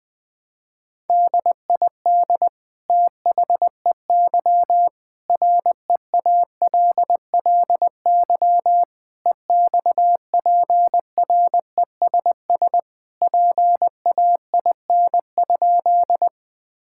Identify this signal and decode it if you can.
Morse code, heard as a keyed tone, about 21 wpm, 700 Hz. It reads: DID THEY REALLY EXPRESS PAIN?
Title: Sentences from Top 1600 Words 20wpm